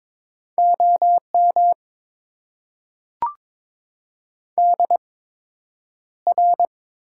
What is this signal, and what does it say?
Morse code, heard as a keyed tone, about 22 wpm, spaced as for 9 wpm, 700 Hz. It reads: OM DR
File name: Top 90 QSO Elements - 8x Character Spacing - 22wpm